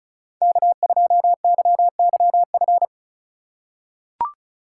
Morse code, heard as a keyed tone, about 34 wpm, 700 Hz. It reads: K2YYF
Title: USA Call Signs - 4x Character Spacing - 35wpm